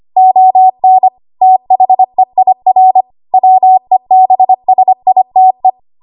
Morse code, heard as a keyed tone, about 25 wpm, 750 Hz. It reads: ON THEIR WEBSITE
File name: Archive / 25 WPM / 25WPM